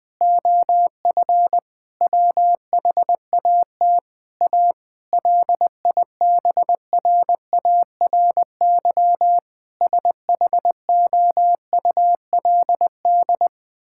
Morse code, heard as a keyed tone, about 20 wpm, 700 Hz. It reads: OF WHAT A LIBRARY SHOULD
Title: Sentences from Top 1000 Words - Review Part 5 - 20wpm